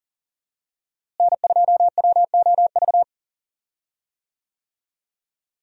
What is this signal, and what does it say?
Morse code, heard as a keyed tone, about 40 wpm, 700 Hz. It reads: N2WOV
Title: USA Call Signs - 5 Characters - Repeated 2x - 40wpm